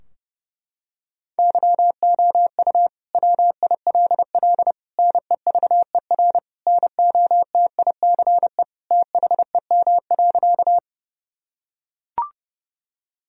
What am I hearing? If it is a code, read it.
Morse code, heard as a keyed tone, about 30 wpm, 700 Hz. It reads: YOU WILL NEVER NOTICE THEM.
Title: Sentences from Top 1000 Words - Review Part 2 - 30wpm